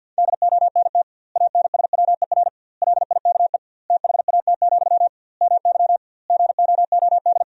Morse code, heard as a keyed tone, about 50 wpm, 700 Hz. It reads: DONT ANSWER LIKE THAT, MY GOOD
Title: 2x Word-Spacing - Sentences from Top 400 Words - 50wpm